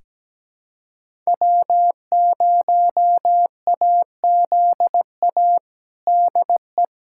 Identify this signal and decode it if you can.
Morse code, heard as a keyed tone, about 17 wpm, 700 Hz. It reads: W0AZA DE
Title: USA Callsign DE Callsign - 5 Characters - 17wpm